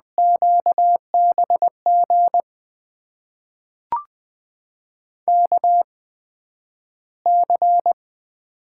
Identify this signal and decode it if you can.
Morse code, heard as a keyed tone, about 20 wpm, spaced as for 11 wpm, 700 Hz. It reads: QBG KC
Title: USA Call Signs - 6 Characters - 8x Character Spacing - 20wpm